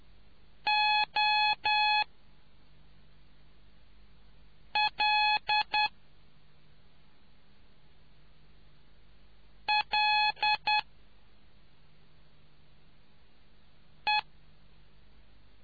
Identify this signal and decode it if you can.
Morse code, heard as a keyed tone, about 10 wpm, 850 Hz. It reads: O L L E